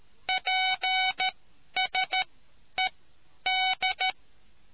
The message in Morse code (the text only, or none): PSED